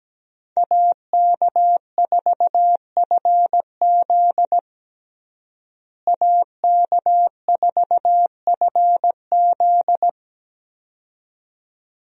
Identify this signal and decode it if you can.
Morse code, heard as a keyed tone, about 17 wpm, 700 Hz. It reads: AK4FZ AK4FZ
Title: USA Call Signs - 5 Characters - Repeated 2x - 17wpm